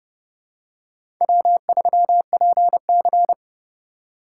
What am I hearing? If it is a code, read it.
Morse code, heard as a keyed tone, about 30 wpm, 700 Hz. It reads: W3PC
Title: USA Call Signs - 4 Characters - 2x Character Spacing - 30wpm